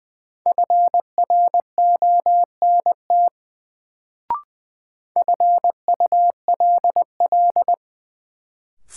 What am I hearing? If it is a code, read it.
Morse code, heard as a keyed tone, about 20 wpm, 700 Hz. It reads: FRONT FULL
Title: Sentences from Top 500 Words - New Word Drill - 20wpm